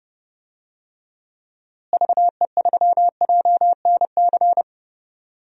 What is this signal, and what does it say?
Morse code, heard as a keyed tone, about 30 wpm, 700 Hz. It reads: VE3JNC
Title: DXCC-100 Call Signs - 6 Characters - Repeated 2x - 30wpm